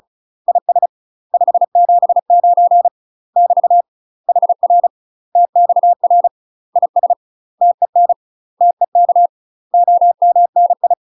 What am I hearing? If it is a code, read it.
Morse code, heard as a keyed tone, about 34 wpm, 700 Hz. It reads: IS 579 <BT> HR TXR IS TEN TEK OMNI
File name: UK MASTER QSOs TRAINING / 35WPM / UK MASTER QSOs 0040